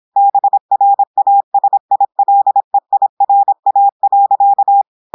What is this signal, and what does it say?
Morse code, heard as a keyed tone, about 26 wpm, 800 Hz. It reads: BRASILEIRA.
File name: audios / portugais / portugais texte en clair 1 vitesse 26 mots minutes